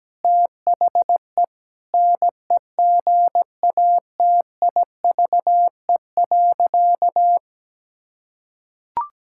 Morse code, heard as a keyed tone, about 17 wpm, 700 Hz. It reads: THE NEGATIVE.
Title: Sentences from Top 1000 Words - Review Part 6 - 17wpm